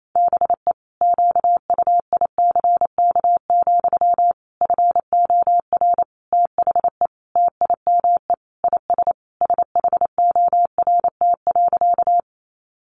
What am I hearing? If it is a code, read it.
Morse code, heard as a keyed tone, about 28 wpm, 700 Hz. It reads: BE QUICK, FOR THE TIME IS SHORT.
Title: Sentences from Top 500 Words 28wpm